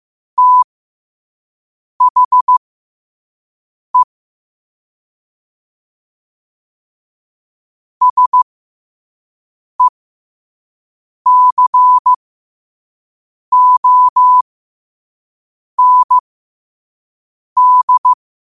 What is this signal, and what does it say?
Morse code, heard as a keyed tone, about 15 wpm, spaced as for 5 wpm, 1000 Hz. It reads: THE SECOND
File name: qotdfiles / 5wpm